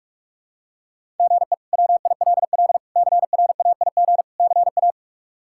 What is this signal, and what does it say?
Morse code, heard as a keyed tone, about 46 wpm, 700 Hz. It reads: GE WILL CRAIG CA
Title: K1USN SST QSO Practice - 45wpm